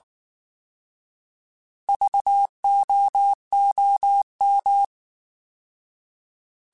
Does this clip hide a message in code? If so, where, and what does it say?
Yes: Morse code from 1.9 to 4.8 s, heard as a keyed tone, about 19 wpm, 800 Hz. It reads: VOOM